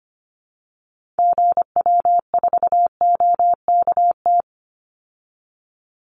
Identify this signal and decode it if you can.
Morse code, heard as a keyed tone, about 25 wpm, 700 Hz. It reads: GW4OKT
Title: Call Signs Repeated 2x - 25wpm